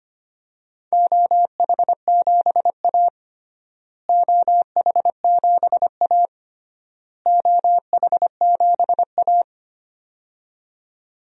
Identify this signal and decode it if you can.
Morse code, heard as a keyed tone, about 25 wpm, 700 Hz. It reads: OH7A OH7A OH7A
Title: DXCC-100 Call Signs - 4 Characters - Repeated 3x - 25wpm